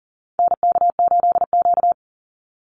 Morse code, heard as a keyed tone, about 40 wpm, 700 Hz. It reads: NK8Q
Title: NA SOTA Activator Callsigns 40wpm